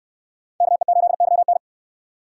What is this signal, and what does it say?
Morse code, heard as a keyed tone, about 64 wpm, 700 Hz. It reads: KE8YN